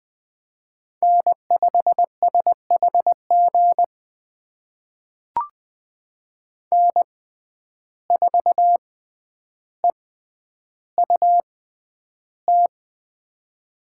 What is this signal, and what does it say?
Morse code, heard as a keyed tone, about 20 wpm, spaced as for 10 wpm, 700 Hz. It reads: N5SHG N4EUT
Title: USA Call Signs - 5 Characters - 6x Character Spacing - 20wpm